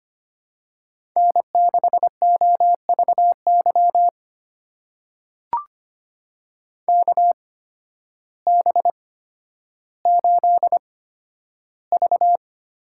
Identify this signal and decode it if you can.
Morse code, heard as a keyed tone, about 25 wpm, spaced as for 12 wpm, 700 Hz. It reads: N6OVY KB8V
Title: USA Call Signs - 8x Character Spacing - 25wpm